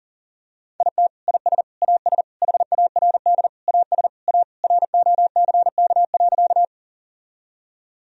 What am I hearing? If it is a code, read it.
Morse code, heard as a keyed tone, about 37 wpm, 700 Hz. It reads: IT IS AS HARD AS A ROCK.
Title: Sentences from Top 1000 Words - Review Part 2 - 40wpm